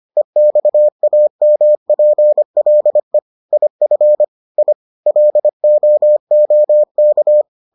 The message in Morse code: EXAMPLE IF I LOOK

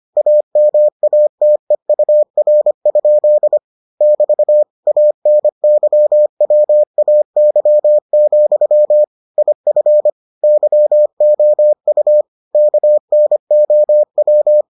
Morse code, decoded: AMATEUR? <BT> ANYWAY, IF YOU KNOW

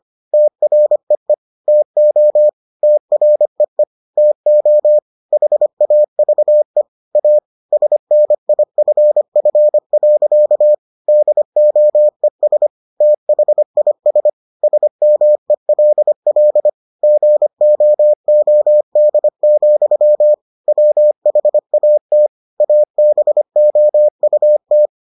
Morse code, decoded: TREE TO TREE TO HAVE A SNIFF. DOES THIS SMELL GOOD, WHAT ABOUT

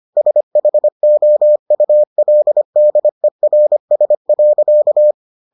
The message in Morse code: SHOULDERS.

